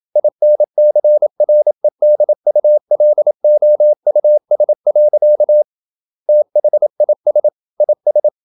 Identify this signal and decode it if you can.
Morse code, heard as a keyed tone, about 27 wpm, 600 Hz. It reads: INCREDULOUS. THIS IS
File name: Never 0027wpm